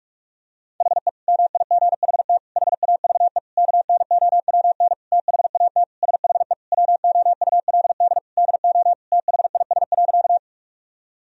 Morse code, heard as a keyed tone, about 47 wpm, 700 Hz. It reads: HE MIGHT HAVE KNOWN THAT SHE WOULD DO THIS.
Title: Sentences from Top 1000 Words - Review Part 3 - 45wpm